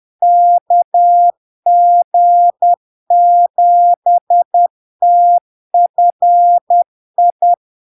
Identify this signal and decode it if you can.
Morse code, heard as a keyed tone, about 10 wpm, 700 Hz. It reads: KG7TFI
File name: LARK-FD-2022-Log-in-CW-10wpm-700-Hz